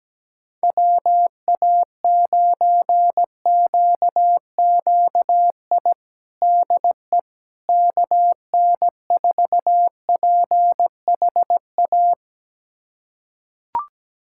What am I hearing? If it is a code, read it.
Morse code, heard as a keyed tone, about 17 wpm, 700 Hz. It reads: WA9QQI DE KN4PHA
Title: USA Callsign DE Callsign 17wpm